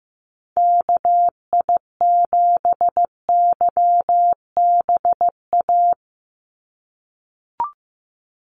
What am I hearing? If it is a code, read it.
Morse code, heard as a keyed tone, about 15 wpm, 700 Hz. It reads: KI7YBA
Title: USA Call Signs 15wpm